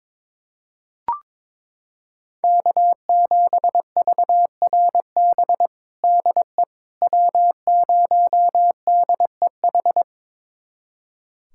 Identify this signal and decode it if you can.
Morse code, heard as a keyed tone, about 22 wpm, 700 Hz. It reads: K7VRB DE W0DEH
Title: USA Callsign DE Callsign - 5 Characters - 22wpm